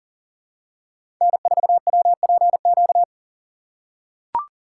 Morse code, heard as a keyed tone, about 40 wpm, 700 Hz. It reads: N4WPQ
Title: USA Call Signs - 5 Characters - 3x Character Spacing - 40wpm